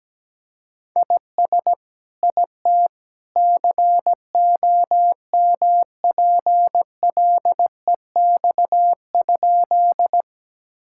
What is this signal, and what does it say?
Morse code, heard as a keyed tone, about 17 wpm, 700 Hz. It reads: IS IT COMPLEX?
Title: Sentences from Top 1000 Words - Review Part 6 - 17wpm